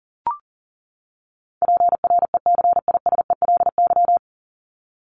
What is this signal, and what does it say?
Morse code, heard as a keyed tone, about 40 wpm, 700 Hz. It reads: PRECISELY
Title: LY Suffix - Top 100 Words with Base Words 40wpm